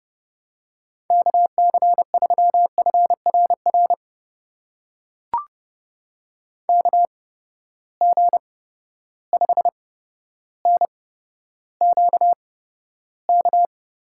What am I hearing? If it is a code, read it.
Morse code, heard as a keyed tone, about 32 wpm, spaced as for 13 wpm, 700 Hz. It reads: KC3FRR KG5NQK
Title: USA Call Signs - 6 Characters - 8x Character Spacing - 30wpm